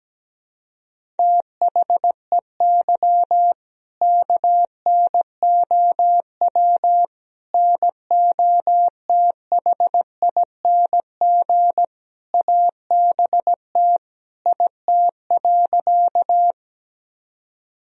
Morse code, heard as a keyed tone, about 17 wpm, 700 Hz. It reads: THEY KNOW NOTHING ABT IT.
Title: Sentences from Top 500 Words and Abbreviations 17wpm